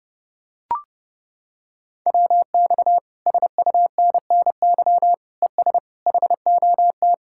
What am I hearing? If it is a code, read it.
Morse code, heard as a keyed tone, about 28 wpm, 700 Hz. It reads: WX SUNNY ES HOT